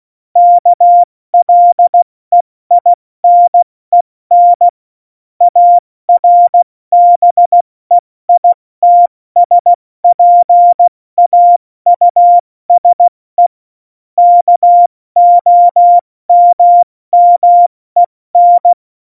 Morse code, heard as a keyed tone, about 16 wpm, spaced as for 14 wpm, 700 Hz. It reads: KLEINEN ARBEITSPAUSE KOMMEN
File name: Morsekurs / Witze / Witz 12 Mai, 2020 70 F80